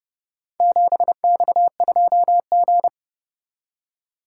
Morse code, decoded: 7X2G

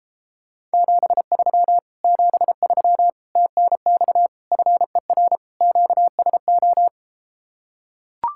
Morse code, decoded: 73 73 TNX FER QSO

